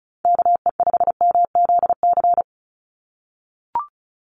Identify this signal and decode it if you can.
Morse code, heard as a keyed tone, about 34 wpm, 700 Hz. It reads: KE5MZC
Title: USA Call Signs 35wpm